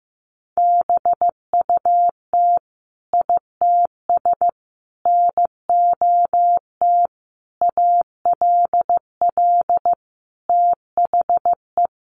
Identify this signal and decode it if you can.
Morse code, heard as a keyed tone, about 15 wpm, 700 Hz. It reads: BUT ITS NOT ALL THE
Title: Sentences from Top 600 Words 15wpm